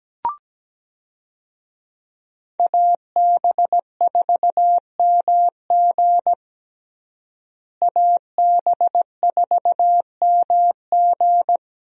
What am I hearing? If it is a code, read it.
Morse code, heard as a keyed tone, about 17 wpm, 700 Hz. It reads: AB4MG AB4MG
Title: USA Call Signs - 5 Characters - Repeated 2x - 17wpm